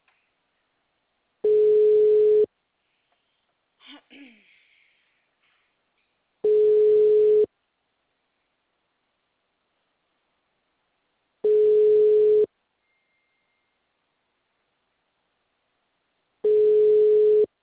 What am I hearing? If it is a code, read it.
Ringback tone (Europe, 425 Hz).